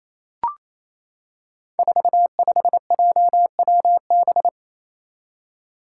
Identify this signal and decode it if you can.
Morse code, heard as a keyed tone, about 28 wpm, 700 Hz. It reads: 45JWB
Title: Five Challenging Letter-Number Combinations 28wpm